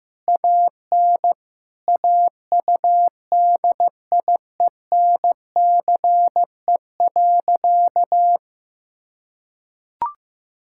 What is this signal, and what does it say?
Morse code, heard as a keyed tone, about 15 wpm, 700 Hz. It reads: AN AUDIENCE.
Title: Sentences from Top 1000 Words - Review Part 5 - 15wpm